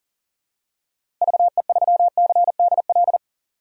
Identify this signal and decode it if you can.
Morse code, heard as a keyed tone, about 40 wpm, 700 Hz. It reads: VE3CDL
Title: DXCC-100 Call Signs - 6 Characters - Repeated 2x - 40wpm